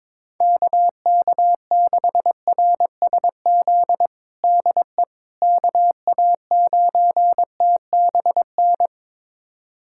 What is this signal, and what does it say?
Morse code, heard as a keyed tone, about 22 wpm, 700 Hz. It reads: KK6RSZ DE KA9TBN
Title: USA Callsign DE Callsign 22wpm